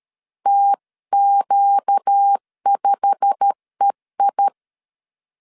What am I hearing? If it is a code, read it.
Morse code, heard as a keyed tone, about 13 wpm, 800 Hz. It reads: TQ5EI
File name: CW / DET / 1 9 1500C S A 12wpm 3800